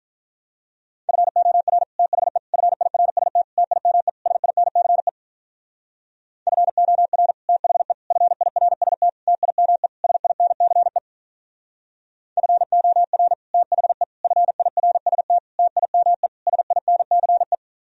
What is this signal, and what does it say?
Morse code, heard as a keyed tone, about 46 wpm, 700 Hz. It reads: FOR THE FIRST TIME SINCE FOR THE FIRST TIME SINCE FOR THE FIRST TIME SINCE